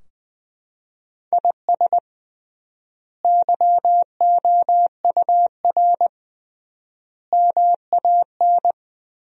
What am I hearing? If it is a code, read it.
Morse code, heard as a keyed tone, about 20 wpm, 700 Hz. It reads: IS YOUR MAN
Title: 3x Word-Spacing - Sentences from Top 400 Words - 20wpm